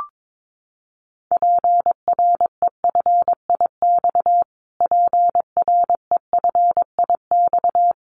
Morse code, heard as a keyed tone, about 22 wpm, 700 Hz. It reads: PREFIX PREFIX
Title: Callsign Prefixes for Top 25 DXCC - Repeated 3x - 22wpm